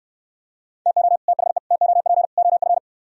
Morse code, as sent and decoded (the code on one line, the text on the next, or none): - --- / - .... / - --- .--. / --- ..-.
TO THE TOP OF